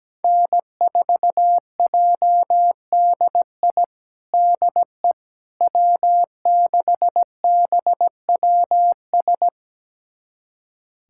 Morse code, heard as a keyed tone, about 17 wpm, 700 Hz. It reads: N4JDI DE W6BWS
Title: USA Callsign DE Callsign - 5 Characters - 17wpm